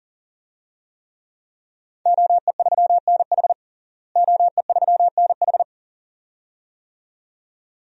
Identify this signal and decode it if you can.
Morse code, heard as a keyed tone, about 37 wpm, 700 Hz. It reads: OE3NH OE3NH
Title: DXCC-100 Call Signs - 5 Characters - Repeated 2x - 40wpm